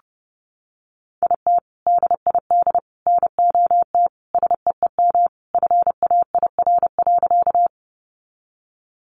Morse code, heard as a keyed tone, about 30 wpm, 700 Hz. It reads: IT DID NOT SEEM FAIR.
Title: Sentences from Top 1300 Words 30wpm